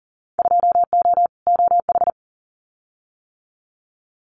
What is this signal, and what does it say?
Morse code, heard as a keyed tone, about 38 wpm, 700 Hz. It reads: 2O OH
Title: Winter Field Day Exchanges 40wpm